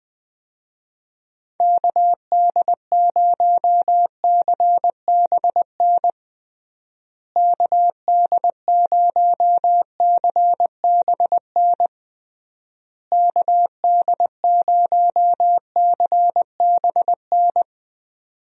Morse code, heard as a keyed tone, about 21 wpm, 700 Hz. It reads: KD0CBN KD0CBN KD0CBN
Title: USA Call Signs - 6 Characters - Repeated 3x - 20wpm